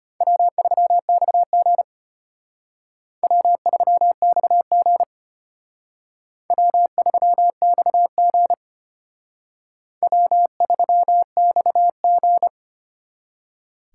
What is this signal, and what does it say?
Morse code, heard as a keyed tone, about 32 wpm, 700 Hz. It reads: W3XG W3XG W3XG W3XG